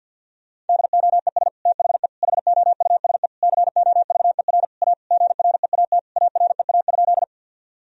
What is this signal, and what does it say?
Morse code, heard as a keyed tone, about 50 wpm, 700 Hz. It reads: DOES THE HOUSE COVER A GREAT AREA?
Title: Sentences from Top 1000 Words - Review Part 2 - 50wpm